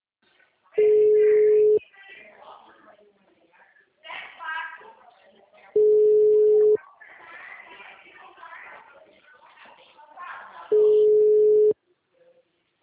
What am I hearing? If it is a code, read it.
Ringback tone (Europe, 425 Hz), with speech.
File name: out- -2009- - - .14256